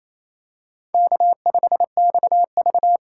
Morse code, heard as a keyed tone, about 28 wpm, 700 Hz. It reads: K5XV
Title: USA Call Signs - 4x Character Spacing - 28wpm